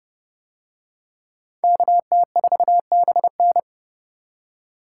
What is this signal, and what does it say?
Morse code, heard as a keyed tone, about 28 wpm, 700 Hz. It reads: KT4BN